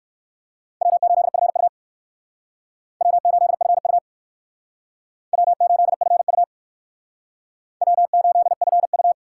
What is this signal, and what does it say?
Morse code, heard as a keyed tone, about 55 wpm, 700 Hz. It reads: W8FU W8FU W8FU W8FU
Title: Speed Racing - USA 1x2 Call Signs 45wpm